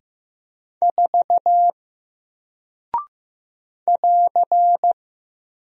Morse code, heard as a keyed tone, about 15 wpm, 700 Hz. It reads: 4 <AR>
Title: Single Character (Letter:Number:Punctuation:Prosign) Alt 15wpm